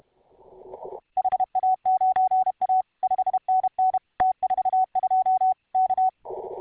Morse code, heard as a keyed tone, about 32 wpm, 750 Hz. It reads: HA9A 5NN T42 K